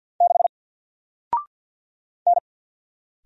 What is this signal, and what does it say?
Morse code, heard as a keyed tone, about 49 wpm, 700 Hz. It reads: 6 N
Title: Single Character (Letter:Number:Punctuation:Prosign) Alt 50wpm